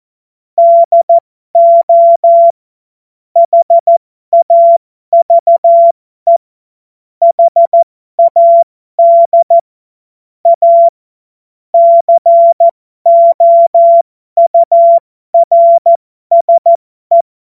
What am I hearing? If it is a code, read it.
Morse code, heard as a keyed tone, about 14 wpm, spaced as for 12 wpm, 680 Hz. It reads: DO HAVE HAD A COURSE